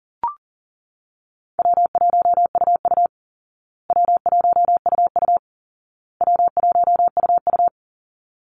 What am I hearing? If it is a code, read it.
Morse code, heard as a keyed tone, about 40 wpm, 700 Hz. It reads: W1UU W1UU W1UU
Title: Easy Call Signs Repeated 3x - 40wpm